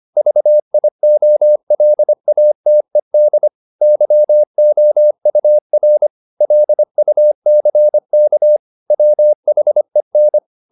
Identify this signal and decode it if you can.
Morse code, heard as a keyed tone, about 25 wpm, 600 Hz. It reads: VIOLATED YOUR LUCK WHEN